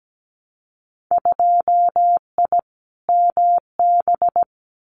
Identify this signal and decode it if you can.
Morse code, heard as a keyed tone, about 17 wpm, 700 Hz. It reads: 2I MB
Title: Winter Field Day Exchanges Encore 17wpm